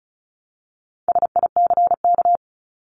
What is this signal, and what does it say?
Morse code, heard as a keyed tone, about 34 wpm, 700 Hz. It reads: SICK
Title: Top 1501-1750 Words 35wpm